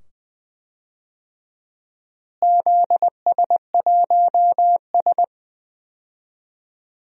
Morse code, heard as a keyed tone, about 20 wpm, 700 Hz. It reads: ZS1S
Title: DXCC-100 Call Signs - 4 Characters - Repeated 2x - 20wpm